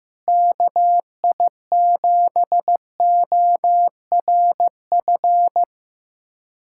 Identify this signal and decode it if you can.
Morse code, heard as a keyed tone, about 15 wpm, 700 Hz. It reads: KI7ORF
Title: USA Callsign DE Callsign 15wpm